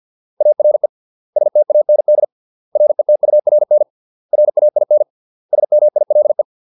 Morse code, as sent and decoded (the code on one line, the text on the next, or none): .- .-. / ... - .- -. -.. / .-. - ..- .-. -. / .-. .- .. -. / ... -- .. .-.. .
ARE STAND RETURN RAIN SMILE